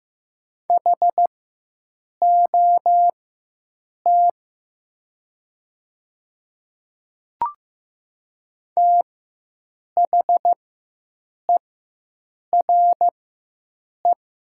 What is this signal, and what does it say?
Morse code, heard as a keyed tone, about 15 wpm, spaced as for 7 wpm, 700 Hz. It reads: HOT THERE
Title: 4x Character Spacing - Top 100 Words - 15wpm